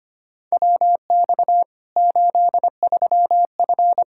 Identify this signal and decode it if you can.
Morse code, heard as a keyed tone, about 25 wpm, 700 Hz. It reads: WX 83F
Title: Practice QSO - Exchange 2 - Info - 25wpm